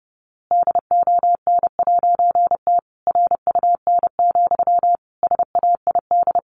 Morse code, heard as a keyed tone, about 30 wpm, 700 Hz. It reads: DON'T RUN, SAID
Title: A Sound Of Thunder 30wpm